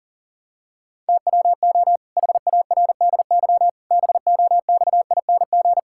Morse code, heard as a keyed tone, about 40 wpm, 700 Hz. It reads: TWO HARDY BOXING
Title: Pangram Challenge v2 - 40wpm